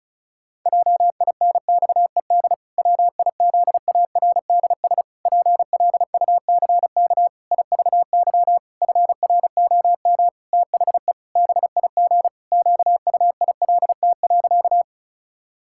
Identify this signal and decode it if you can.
Morse code, heard as a keyed tone, about 34 wpm, 700 Hz. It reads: JINXED WIZARDS PLUCK IVY FROM THE BIG QUILT.
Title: Pangram Challenge v2 - 35wpm